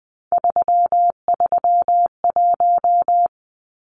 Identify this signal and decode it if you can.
Morse code, heard as a keyed tone, about 20 wpm, 700 Hz. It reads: 331
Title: Numbers - 3-Digit 20wpm